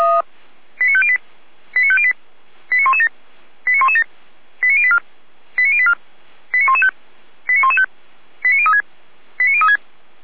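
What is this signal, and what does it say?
Five-tone selective call (ZVEI), heard as quick runs of five tones, tones of about 70 ms. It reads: 89508 89508 89117 89117 89084 89084 89115 89115 89036 89036
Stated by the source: ZVEI five-tone selective call received with an SDR.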